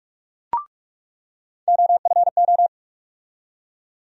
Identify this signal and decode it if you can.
Morse code, heard as a keyed tone, about 46 wpm, 700 Hz. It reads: OFO
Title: USA Call Sign Postfixes - 3 Letters 45wpm